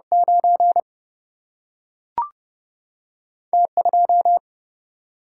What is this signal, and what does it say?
Morse code, heard as a keyed tone, about 30 wpm, 700 Hz. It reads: 9 T2
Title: Single-Letter Single-Number 30wpm